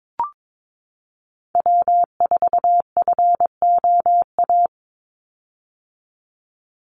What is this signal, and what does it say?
Morse code, heard as a keyed tone, about 22 wpm, 700 Hz. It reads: W4FOA
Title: Call Signs - 22wpm